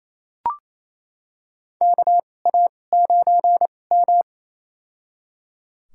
Morse code, heard as a keyed tone, about 27 wpm, spaced as for 20 wpm, 700 Hz. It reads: KA9M
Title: USA Call Signs - 2x Character Spacing - 28wpm